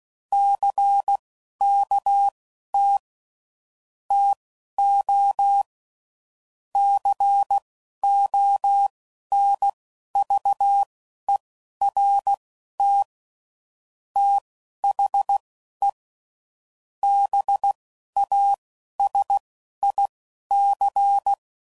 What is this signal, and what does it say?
Morse code, heard as a keyed tone, about 16 wpm, spaced as for 11 wpm, 800 Hz. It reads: CKT TO CONVERT THE BASIC
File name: CW Code Groups @10WPM / Technical Subjects / SwPwrSup